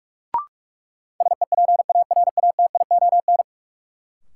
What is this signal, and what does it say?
Morse code, heard as a keyed tone, about 41 wpm, 700 Hz. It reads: SEPARATION
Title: Sets of 5 Words - New Word Drill - 45wpm